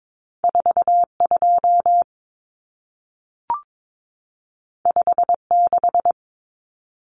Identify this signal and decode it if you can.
Morse code, heard as a keyed tone, about 22 wpm, 700 Hz. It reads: 42 56
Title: Numbers - 2-Digit 22wpm